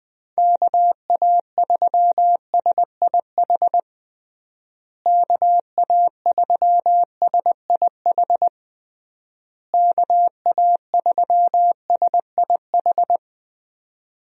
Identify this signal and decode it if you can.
Morse code, heard as a keyed tone, about 20 wpm, 700 Hz. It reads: KA3SIH KA3SIH KA3SIH
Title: USA Call Signs - 6 Characters - Repeated 3x - 20wpm